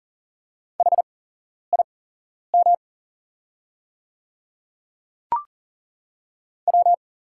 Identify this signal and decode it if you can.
Morse code, heard as a keyed tone, about 40 wpm, spaced as for 11 wpm, 700 Hz. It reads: HIM W
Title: 8x Character Spacing - Top 100 Words - 40wpm